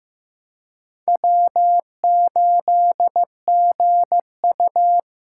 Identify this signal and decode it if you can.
Morse code, heard as a keyed tone, about 15 wpm, 700 Hz. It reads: W8GU
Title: K1USN SST QSO Practice - 15wpm